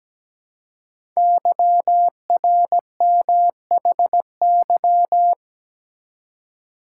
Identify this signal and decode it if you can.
Morse code, heard as a keyed tone, about 17 wpm, 700 Hz. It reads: YRMHY
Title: Code Groups - 5 Letters - 17wpm